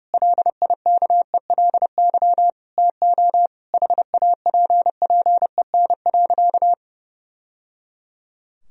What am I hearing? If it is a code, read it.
Morse code, heard as a keyed tone, about 30 wpm, 700 Hz. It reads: LIKELY TO HAPPEN.
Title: Sentences from Top 1000 Words - Review Part 4 - 30wpm